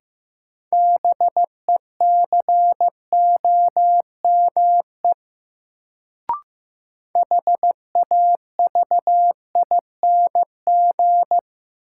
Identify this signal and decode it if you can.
Morse code, heard as a keyed tone, about 15 wpm, 700 Hz. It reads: BECOME HAVING
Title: Sets of 5 Words Encore - New Word Drill - 15wpm